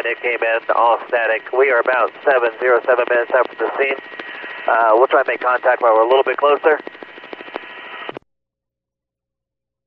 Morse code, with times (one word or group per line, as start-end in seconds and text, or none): none